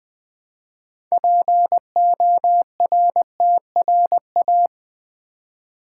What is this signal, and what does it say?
Morse code, heard as a keyed tone, about 20 wpm, 700 Hz. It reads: PORTRA
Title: Build A Word - 3 Letters At A Time - 8-Letter Words - 20wpm